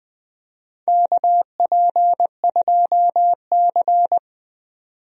Morse code, heard as a keyed tone, about 20 wpm, 700 Hz. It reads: KP2C
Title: DXCC-100 Call Signs - 4 Characters - 20wpm